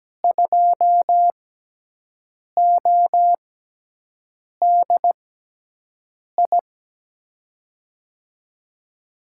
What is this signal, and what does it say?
Morse code, heard as a keyed tone, about 17 wpm, spaced as for 6 wpm, 700 Hz. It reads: 2ODI
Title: USA Call Signs - 5 Characters - 6x Character Spacing - 17wpm